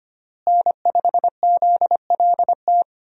Morse code, heard as a keyed tone, about 25 wpm, 700 Hz. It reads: N5ZLT
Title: USA Call Signs - 5 Characters - Repeated 2x - 25wpm